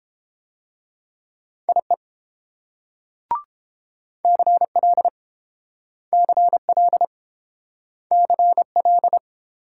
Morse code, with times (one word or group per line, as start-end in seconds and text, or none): 1.7-1.9 IE
4.2-5.1 CL
6.1-7.0 CL
8.1-9.2 CL